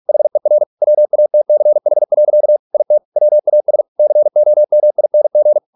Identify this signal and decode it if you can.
Morse code, heard as a keyed tone, about 46 wpm, 600 Hz. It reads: HER WATCH. IT WAS COMING